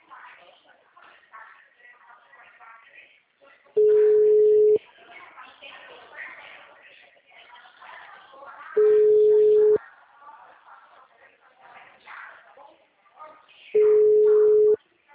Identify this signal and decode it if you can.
Ringback tone (Europe, 425 Hz), with speech.
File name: out- -2012- - - .294